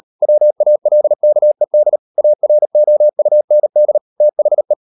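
Morse code, heard as a keyed tone, about 38 wpm, 600 Hz. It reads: WALKED AROUND THE
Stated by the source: Morse code practice at 38 WPM, sending Ken Follett's novel Never in English, from CW med Gnister.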